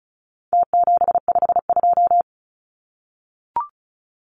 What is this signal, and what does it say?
Morse code, heard as a keyed tone, about 34 wpm, 700 Hz. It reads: T752
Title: Numbers - Contest Style - 4-Digit 35wpm